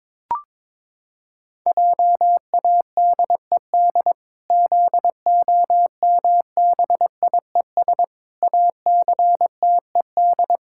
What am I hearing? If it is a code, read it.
Morse code, heard as a keyed tone, about 22 wpm, 700 Hz. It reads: JADED ZOMBIES ACTED
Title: Pangram Challenge v2 - 22wpm